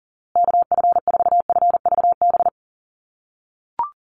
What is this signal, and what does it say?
Morse code, heard as a keyed tone, about 40 wpm, 700 Hz. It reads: KF4FVB
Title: USA Call Signs Repeated 3x 40wpm